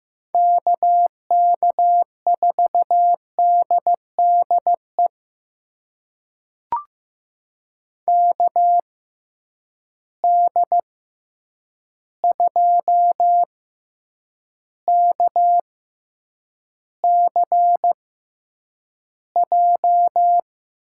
Morse code, heard as a keyed tone, about 15 wpm, spaced as for 8 wpm, 700 Hz. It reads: KK4DDE KD2KCJ
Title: USA Call Signs - 6 Characters - 6x Character Spacing - 15wpm